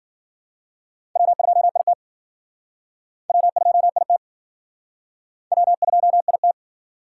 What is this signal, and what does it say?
Morse code, heard as a keyed tone, about 53 wpm, 700 Hz. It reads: W2IT W2IT W2IT